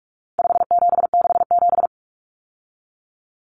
Morse code, heard as a keyed tone, about 44 wpm, 700 Hz. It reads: 5767
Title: Common Cut Numbers - 4-Digit 45wpm